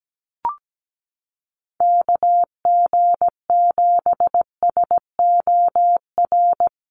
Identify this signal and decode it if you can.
Morse code, heard as a keyed tone, about 17 wpm, 700 Hz. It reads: KG7SOR
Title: USA Call Signs Repeated 2x 17wpm